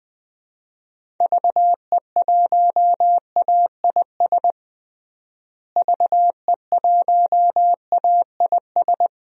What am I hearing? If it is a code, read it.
Morse code, heard as a keyed tone, about 20 wpm, 700 Hz. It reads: VE1AIS VE1AIS
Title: DXCC-100 Call Signs - 6 Characters - Repeated 2x - 20wpm